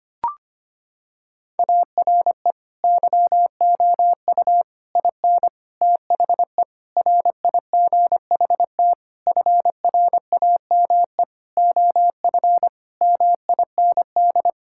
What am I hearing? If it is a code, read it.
Morse code, heard as a keyed tone, about 25 wpm, 700 Hz. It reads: ARE YOU IN THE RIGHT FRAME OF MIND
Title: Sentences from Top 1000 Words - Review Part 6 - 25wpm